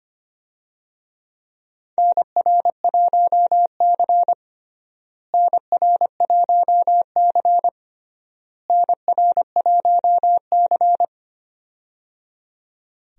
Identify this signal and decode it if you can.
Morse code, heard as a keyed tone, about 25 wpm, 700 Hz. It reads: NR1C NR1C NR1C
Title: USA Call Signs - 4 Characters - Repeated 3x - 25wpm